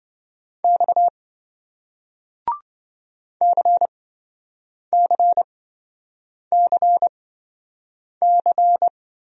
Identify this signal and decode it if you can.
Morse code, heard as a keyed tone, about 27 wpm, spaced as for 7 wpm, 700 Hz. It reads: X CCCC